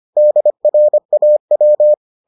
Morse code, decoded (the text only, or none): DRAW